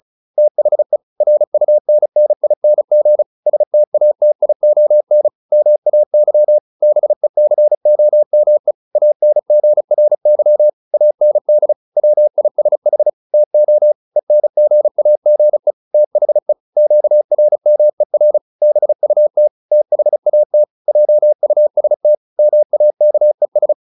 Morse code, heard as a keyed tone, about 34 wpm, 600 Hz. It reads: THE RUNNING STATION MAY BECOME ANGRY AND WISH TO ENGAGE THE QRMER BUT THAT JUST MAKES
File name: SS309 35